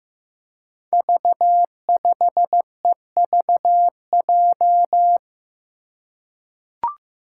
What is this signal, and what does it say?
Morse code, heard as a keyed tone, about 15 wpm, 700 Hz. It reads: V5EVJ
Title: Five Challenging Letter-Number Combinations 15wpm